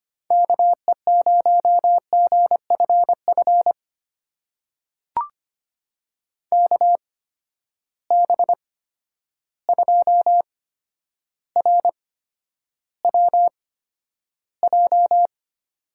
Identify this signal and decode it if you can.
Morse code, heard as a keyed tone, about 25 wpm, spaced as for 11 wpm, 700 Hz. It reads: KE0GFF KB2RWJ
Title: USA Call Signs - 6 Characters - 8x Character Spacing - 25wpm